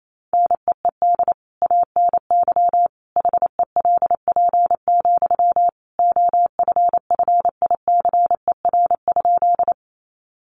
Morse code, heard as a keyed tone, about 28 wpm, 700 Hz. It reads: NEED ANY HELP, OFFICER?
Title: Sentences from Top 1200 Words 28wpm